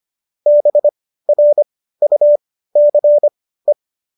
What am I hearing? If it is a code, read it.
Morse code, heard as a keyed tone, about 25 wpm, spaced as for 15 wpm, 590 Hz. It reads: BRUCE
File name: qso201 15